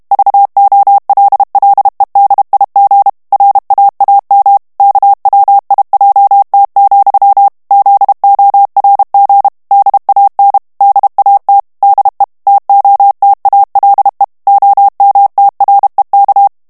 Morse code, heard as a keyed tone, about 32 wpm, 800 Hz. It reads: VOLLEDIG RAAM KWIJT, ZORG DAN DAT DE TOTALE OMTREK
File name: morse mp3 / HOLLANDAIS texte clair 1 vitesse 30 mots minute